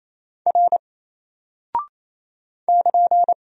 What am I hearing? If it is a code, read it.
Morse code, heard as a keyed tone, about 27 wpm, 700 Hz. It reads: R <KN>
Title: Single Character (Letter:Number:Punctuation:Prosign) 28wpm